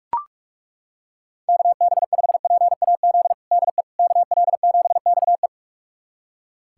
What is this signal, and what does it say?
Morse code, heard as a keyed tone, about 42 wpm, 700 Hz. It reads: KB5PAZ DE KL7XE
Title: USA Callsign DE Callsign 45wpm